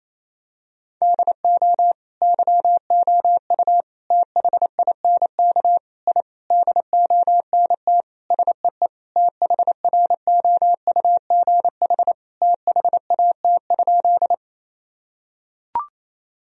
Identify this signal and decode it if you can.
Morse code, heard as a keyed tone, about 28 wpm, 700 Hz. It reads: DO YOU THINK I DONT SEE THROUGH THAT?
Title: Sentences from Top 1000 Words - Review Part 1 - 28wpm